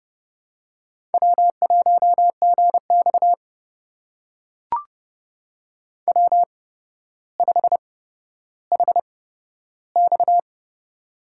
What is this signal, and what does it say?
Morse code, heard as a keyed tone, about 31 wpm, spaced as for 13 wpm, 700 Hz. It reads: W1GX W5HX
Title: USA Call Signs - 8x Character Spacing - 30wpm